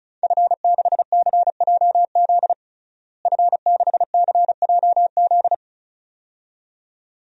Morse code, heard as a keyed tone, about 36 wpm, 700 Hz. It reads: F6CJZ F6CJZ
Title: DXCC-100 Call Signs - 5 Characters - Repeated 2x - 35wpm